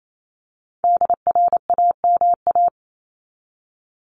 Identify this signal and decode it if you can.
Morse code, heard as a keyed tone, about 28 wpm, 700 Hz. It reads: DRAMA
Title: Top 1251-1500 Words 28wpm